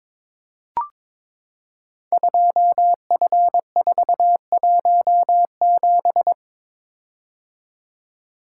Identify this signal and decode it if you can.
Morse code, heard as a keyed tone, about 22 wpm, 700 Hz. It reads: 2F417
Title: Code Groups - 5 Letter:Numbers - 22wpm